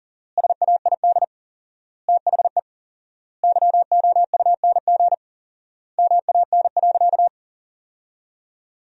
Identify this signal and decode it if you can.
Morse code, heard as a keyed tone, about 40 wpm, 700 Hz. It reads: SAID THE YOUNG MAN.
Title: 4x Word-Spacing - Sentences from Top 400 Words - 40wpm